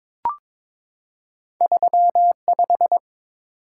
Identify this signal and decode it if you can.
Morse code, heard as a keyed tone, about 22 wpm, 700 Hz. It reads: 35
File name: practice / 22wpm / Two Letter-Number Combinations - 22wpm